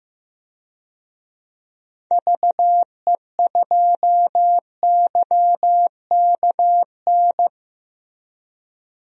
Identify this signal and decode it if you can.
Morse code, heard as a keyed tone, about 15 wpm, 700 Hz. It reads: VE2YKN
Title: DXCC-100 Call Signs - 6 Characters - Repeated 3x - 15wpm